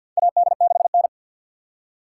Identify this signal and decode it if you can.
Morse code, heard as a keyed tone, about 46 wpm, 700 Hz. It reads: AD6N
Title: USA Call Signs - 4 Characters - 2x Character Spacing - 50wpm